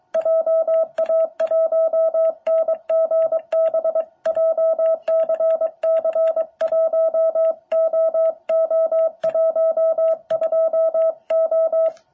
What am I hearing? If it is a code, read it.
Morse code, heard as a keyed tone, about 22 wpm, 650 Hz. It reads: JA1NGB JCC1OO12O
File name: jh8cbh / 2021cw